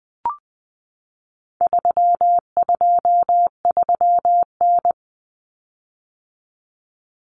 Morse code, heard as a keyed tone, about 20 wpm, 700 Hz. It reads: 323N
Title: Common Cut Numbers - 4-Digit 20wpm